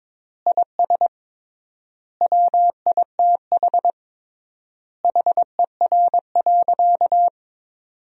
Morse code, heard as a keyed tone, about 23 wpm, 700 Hz. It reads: IS WITH HER.
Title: 3x Word-Spacing - Sentences from Top 100 Words - 22wpm